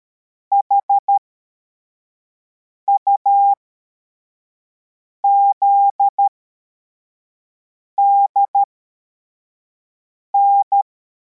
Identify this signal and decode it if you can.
Morse code, heard as a keyed tone, about 13 wpm, spaced as for 4 wpm, 800 Hz. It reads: HUZDN